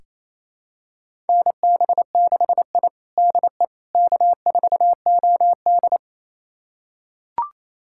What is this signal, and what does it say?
Morse code, heard as a keyed tone, about 28 wpm, 700 Hz. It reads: NB6I DE K4OD
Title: USA Callsign DE Callsign - 4 Characters - 28wpm